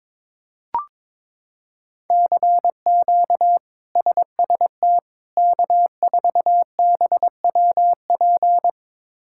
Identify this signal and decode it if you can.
Morse code, heard as a keyed tone, about 22 wpm, 700 Hz. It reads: CQ SST K4BWP
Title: K1USN SST QSO Practice - 22wpm